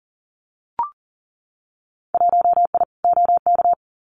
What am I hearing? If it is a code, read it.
Morse code, heard as a keyed tone, about 40 wpm, 700 Hz. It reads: 1I OK
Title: Winter Field Day Exchanges Encore 40wpm